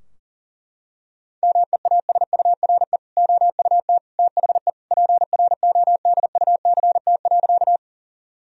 Morse code, heard as a keyed tone, about 40 wpm, 700 Hz. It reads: MEASURE OUT THE PRODUCT.
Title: Sentences from Top 1000 Words - Review Part 2 - 40wpm